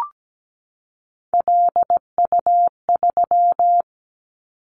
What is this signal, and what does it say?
Morse code, heard as a keyed tone, about 17 wpm, 700 Hz. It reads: LU3
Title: Callsign Prefixes for Top 50 DXCC - 17wpm